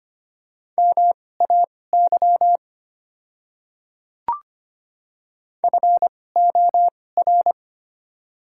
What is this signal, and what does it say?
Morse code, heard as a keyed tone, about 25 wpm, spaced as for 18 wpm, 690 Hz. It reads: MAY FOR
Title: 2x Character Spacing - Top 300 Words - 25wpm